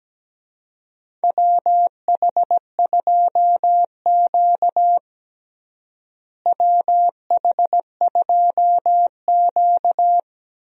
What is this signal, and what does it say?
Morse code, heard as a keyed tone, about 17 wpm, 700 Hz. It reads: WH2Q WH2Q
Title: USA Call Signs - 4 Characters - Repeated 3x - 17wpm